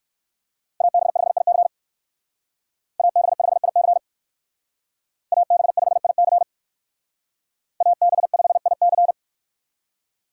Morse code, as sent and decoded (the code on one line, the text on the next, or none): .- -... ..... .. -.-. / .- -... ..... .. -.-. / .- -... ..... .. -.-. / .- -... ..... .. -.-.
AB5IC AB5IC AB5IC AB5IC